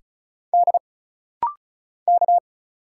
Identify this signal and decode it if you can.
Morse code, heard as a keyed tone, about 34 wpm, 700 Hz. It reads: D K
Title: Single Letters - Rapid-Fire 35wpm